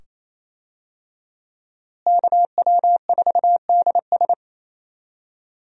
Morse code, heard as a keyed tone, about 28 wpm, 700 Hz. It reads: KW4DS
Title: USA Call Signs - 5 Characters - Repeated 2x - 28wpm